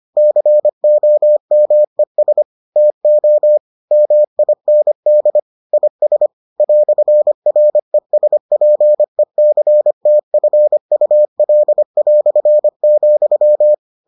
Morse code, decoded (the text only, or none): COMES TO MIND IS "RESPECTFUL",